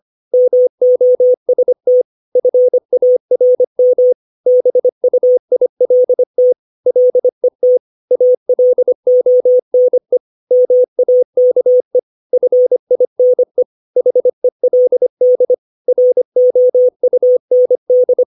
Morse code, heard as a keyed tone, about 25 wpm, 500 Hz. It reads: MOST FARM BUILT LET ALONE MAKE FINE HELD ROUND
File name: 632 words / 632 Words 25WPM